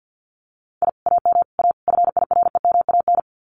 Morse code, heard as a keyed tone, about 50 wpm, 700 Hz. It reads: I AM A FIREMAN
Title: Common QSO Phrases Encore 50wpm